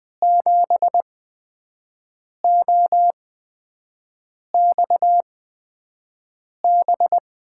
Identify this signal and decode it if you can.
Morse code, heard as a keyed tone, about 21 wpm, spaced as for 5 wpm, 700 Hz. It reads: 7OXB
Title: USA Call Signs - 6 Characters - 8x Character Spacing - 20wpm